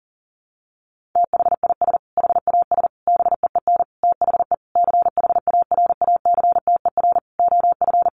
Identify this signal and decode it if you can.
Morse code, heard as a keyed tone, about 40 wpm, 700 Hz. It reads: THIS HAS BEEN THE CHARACTER OF
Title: Sentences from Top 1400 Words 40wpm